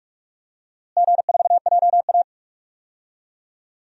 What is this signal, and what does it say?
Morse code, heard as a keyed tone, about 46 wpm, 700 Hz. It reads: G4JA
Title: DXCC-100 Call Signs - 4 Characters - Repeated 2x - 45wpm